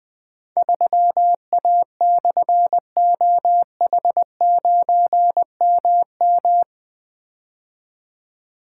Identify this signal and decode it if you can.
Morse code, heard as a keyed tone, about 20 wpm, 700 Hz. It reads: 3A/OH9MM
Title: Difficult Call Signs 20wpm